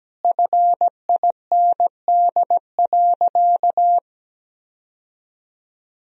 Morse code, heard as a keyed tone, about 17 wpm, 700 Hz. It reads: FIND.
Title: Sentences from Top 1000 Words - Review Part 2 - 17wpm